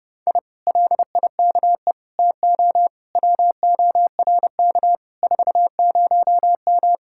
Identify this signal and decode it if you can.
Morse code, heard as a keyed tone, about 30 wpm, 700 Hz. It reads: I LIKE TO WORK 40M